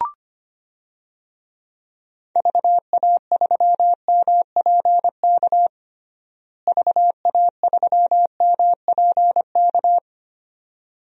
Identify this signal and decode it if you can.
Morse code, heard as a keyed tone, about 25 wpm, 700 Hz. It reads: VA3MPK VA3MPK
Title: DXCC-100 Call Signs - 6 Characters - Repeated 2x - 25wpm